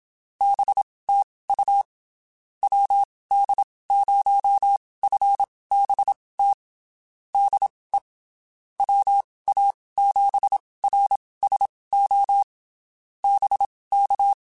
Morse code, heard as a keyed tone, about 27 wpm, spaced as for 19 wpm, 800 Hz. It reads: BTU WD0FBT DE WA7RSO BK
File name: (spaced) / QSOs Spaced-mp3s / WD0FBT @25 sp